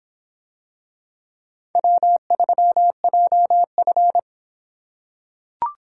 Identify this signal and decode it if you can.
Morse code, heard as a keyed tone, about 23 wpm, 700 Hz. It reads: W3JF